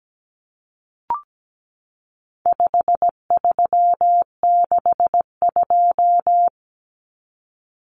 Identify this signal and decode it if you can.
Morse code, heard as a keyed tone, about 17 wpm, 700 Hz. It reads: 5362
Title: Common Cut Numbers - 4-Digit 17wpm